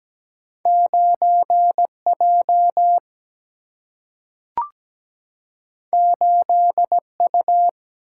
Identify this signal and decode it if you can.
Morse code, heard as a keyed tone, about 17 wpm, 700 Hz. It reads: 9 J 8 U
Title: Single-Letter Single-Number 17wpm